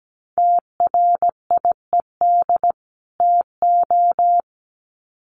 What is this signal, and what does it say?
Morse code, heard as a keyed tone, about 17 wpm, 700 Hz. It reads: TRIED TO
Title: Sets of 2 Words 17wpm